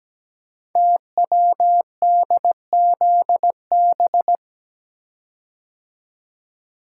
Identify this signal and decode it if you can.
Morse code, heard as a keyed tone, about 17 wpm, 700 Hz. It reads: TWDZB
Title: Code Groups - 5 Letters - 17wpm